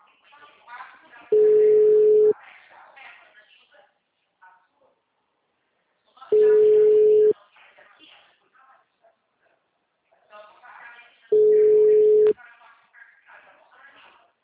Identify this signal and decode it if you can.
Ringback tone (Europe, 425 Hz), with speech.